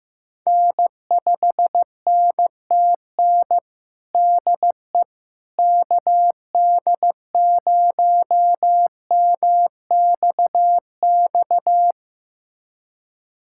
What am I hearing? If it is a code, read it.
Morse code, heard as a keyed tone, about 15 wpm, 700 Hz. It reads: N5NTN DE KD0MXX
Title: USA Callsign DE Callsign 15wpm